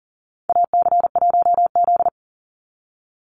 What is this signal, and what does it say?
Morse code, heard as a keyed tone, about 40 wpm, 700 Hz. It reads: AC1Z
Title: NA SOTA Activator Callsigns 40wpm